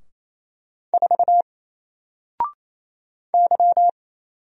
Morse code, heard as a keyed tone, about 27 wpm, 700 Hz. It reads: 4 Y
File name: Single-Letter-Number-28wpm